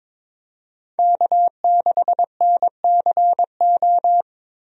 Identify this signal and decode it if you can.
Morse code, heard as a keyed tone, about 22 wpm, 700 Hz. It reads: K6NCO
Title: USA Call Signs - 5 Characters - 22wpm